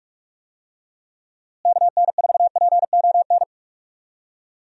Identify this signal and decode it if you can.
Morse code, heard as a keyed tone, about 46 wpm, 700 Hz. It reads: KN4PON
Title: USA Call Signs - 6 Characters - Repeated 3x - 45wpm